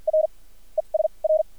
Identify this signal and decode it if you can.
Morse code, heard as a keyed tone, about 50 wpm, spaced as for 37 wpm, 630 Hz. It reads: R ES R
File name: archived / CRSnet